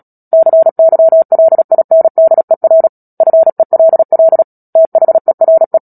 Morse code, heard as a keyed tone, about 37 wpm, 650 Hz. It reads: CYLINDER FELL THERE
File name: Book Two / 10-Chapter 10 - Epilogue - 36 WPM